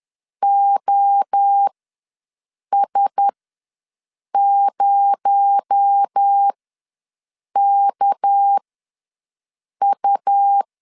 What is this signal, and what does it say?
Morse code, heard as a keyed tone, about 10 wpm, spaced as for 6 wpm, 800 Hz. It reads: OS0KU
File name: LEZIONI / 7 GF9Z / 7 G ws 10wpm 3843